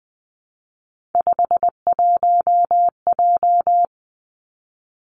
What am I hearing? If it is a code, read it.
Morse code, heard as a keyed tone, about 20 wpm, 700 Hz. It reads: 51J
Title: Three Challenging Letter-Number Combinations 20wpm